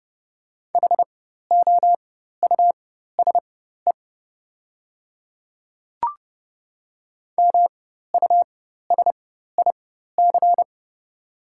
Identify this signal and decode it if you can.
Morse code, heard as a keyed tone, about 30 wpm, spaced as for 14 wpm, 700 Hz. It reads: HOUSE MUSIC
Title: 4x Character Spacing - Top 300 Words - 30wpm